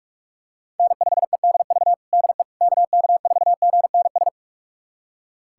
Morse code, heard as a keyed tone, about 46 wpm, 700 Hz. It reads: N5EDV DE KK4GNS
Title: USA Callsign DE Callsign 45wpm